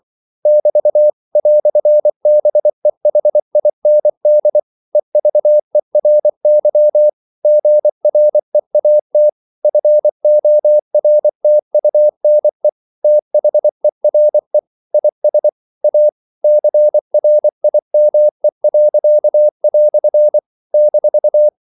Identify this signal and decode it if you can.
Morse code, heard as a keyed tone, about 24 wpm, 600 Hz. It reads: <BT> "BEHIND EVERY GREAT FORTUNE THERE IS A CRIME." -
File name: CW / Practice / EnglishQuotations0324